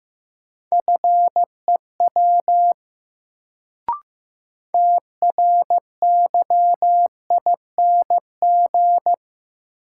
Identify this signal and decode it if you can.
Morse code, heard as a keyed tone, about 15 wpm, 700 Hz. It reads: FEW TRYING
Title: Sets of 2 Words - New Word Drill - 15wpm